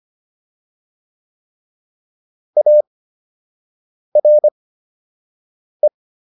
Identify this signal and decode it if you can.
Morse code, heard as a keyed tone, about 25 wpm, spaced as for 6 wpm, 600 Hz. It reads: ARE